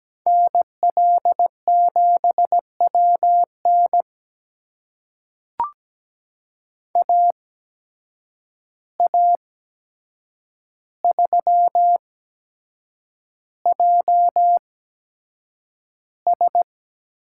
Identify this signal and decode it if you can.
Morse code, heard as a keyed tone, about 17 wpm, spaced as for 7 wpm, 700 Hz. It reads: NL7WN AA3JS
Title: USA Call Signs - 5 Characters - 8x Character Spacing - 17wpm